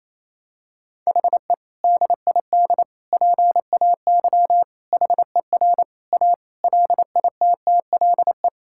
Morse code, heard as a keyed tone, about 28 wpm, 700 Hz. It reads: HE DID PAY HER A LITTLE
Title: Sentences from Top 1000 Words - Review Part 6 - 28wpm